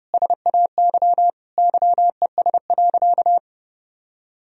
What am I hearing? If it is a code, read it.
Morse code, heard as a keyed tone, about 30 wpm, 700 Hz. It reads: SAY YES.
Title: Sentences from Top 500 Words and Abbreviations 30wpm